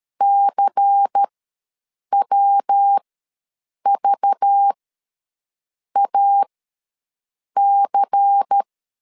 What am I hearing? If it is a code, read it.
Morse code, heard as a keyed tone, about 13 wpm, spaced as for 6 wpm, 800 Hz. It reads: CWVAC
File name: LEZIONI / 5 LBVC / 5 LBVC ws 12wpm 3159